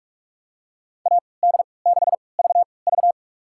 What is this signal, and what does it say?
Morse code, heard as a keyed tone, about 46 wpm, spaced as for 26 wpm, 700 Hz. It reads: AD6VV
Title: USA Call Signs - 3x Character Spacing - 45wpm